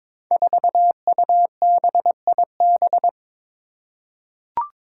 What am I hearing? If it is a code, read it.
Morse code, heard as a keyed tone, about 22 wpm, 700 Hz. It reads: 4UBIB
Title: Five Challenging Letter-Number Combinations 22wpm